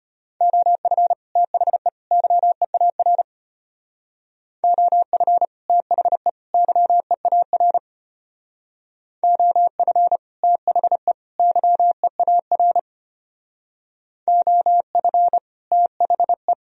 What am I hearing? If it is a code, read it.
Morse code, heard as a keyed tone, about 32 wpm, 700 Hz. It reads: OF THE YEAR OF THE YEAR OF THE YEAR OF THE